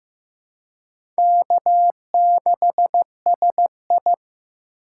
Morse code, heard as a keyed tone, about 15 wpm, 700 Hz. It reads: K6SI
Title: USA Call Signs - 2x Character Spacing - 15wpm